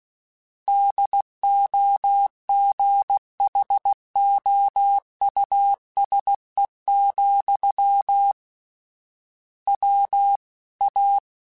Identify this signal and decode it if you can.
Morse code, heard as a keyed tone, about 16 wpm, 800 Hz. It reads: DOGHOUSE, W A
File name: General / General-mp3s / Houses @15